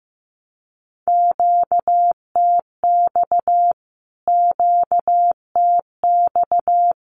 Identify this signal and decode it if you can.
Morse code, heard as a keyed tone, about 15 wpm, 700 Hz. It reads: QTX QTX
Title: Q-Codes Repeated 3x 15wpm